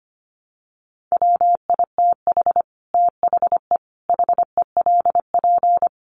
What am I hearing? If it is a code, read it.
Morse code, heard as a keyed tone, about 25 wpm, 700 Hz. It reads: WITH THE HELP
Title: Sets of 5 Words Encore 25wpm